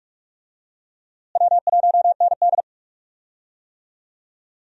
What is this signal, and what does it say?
Morse code, heard as a keyed tone, about 42 wpm, 700 Hz. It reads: W1ND
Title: USA Call Signs - 4 Characters - Repeated 2x - 45wpm